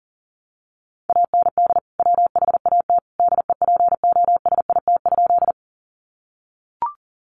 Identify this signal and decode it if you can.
Morse code, heard as a keyed tone, about 40 wpm, 700 Hz. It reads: AND WHAT DEPOSIT?
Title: Sentences from Top 1300 Words 40wpm